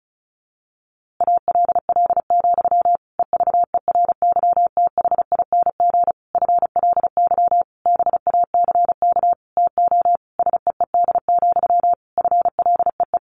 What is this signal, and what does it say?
Morse code, heard as a keyed tone, about 34 wpm, 700 Hz. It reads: ALL, EVERYTHING FLY BACK TO SEED, FLEE
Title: A Sound Of Thunder 35wpm